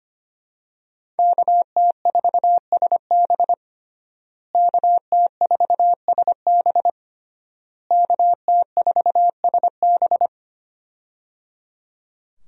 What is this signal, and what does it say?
Morse code, heard as a keyed tone, about 25 wpm, 700 Hz. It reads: KT4SB KT4SB KT4SB
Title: USA Call Signs - 5 Characters - Repeated 3x - 25wpm